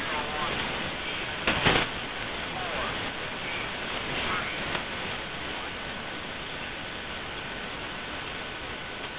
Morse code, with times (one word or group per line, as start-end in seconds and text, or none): none